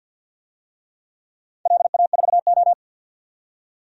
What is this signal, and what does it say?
Morse code, heard as a keyed tone, about 53 wpm, 700 Hz. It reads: LA4O
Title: DXCC-100 Call Signs - 4 Characters - 50wpm